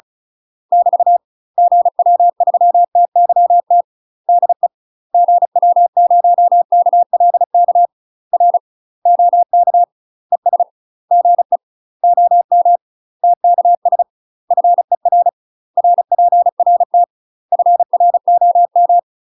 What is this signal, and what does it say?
Morse code, heard as a keyed tone, about 33 wpm, 700 Hz. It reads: <BT> GW3TYT DE GW0KLK R OK ES GE OM TKS FER RPRT FROM
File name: UK MASTER QSOs TRAINING / 35WPM / UK MASTER QSOs 0036